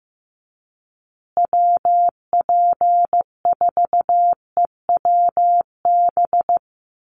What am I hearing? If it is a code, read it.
Morse code, heard as a keyed tone, about 15 wpm, 700 Hz. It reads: WP4EWB
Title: USA Call Signs Repeated 2x 15wpm